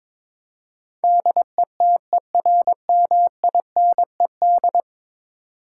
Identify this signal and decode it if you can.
Morse code, heard as a keyed tone, about 22 wpm, 700 Hz. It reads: DETERMINED
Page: morse-code-ninja-practice-22wpm